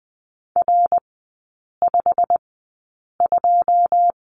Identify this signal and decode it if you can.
Morse code, heard as a keyed tone, about 20 wpm, spaced as for 8 wpm, 700 Hz. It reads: R52
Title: 114 Lesson 19e - Letter w - Three Characters 20wpm